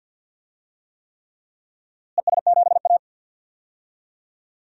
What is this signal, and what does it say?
Morse code, heard as a keyed tone, about 52 wpm, 700 Hz. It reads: ES7A
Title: DXCC-100 Call Signs - 4 Characters - Repeated 3x - 50wpm